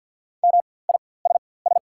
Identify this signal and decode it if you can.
Morse code, heard as a keyed tone, about 48 wpm, spaced as for 23 wpm, 690 Hz. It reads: MISS
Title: 4x Character Spacing - Top 500 Words - 50wpm